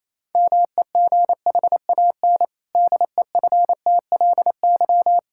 Morse code, heard as a keyed tone, about 28 wpm, 700 Hz. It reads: MEGHAN DEFTLY
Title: Pangram Challenge v2 - 28wpm